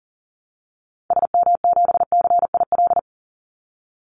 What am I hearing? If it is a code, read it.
Morse code, heard as a keyed tone, about 42 wpm, 700 Hz. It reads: SM7CIL
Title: Call Signs - 40wpm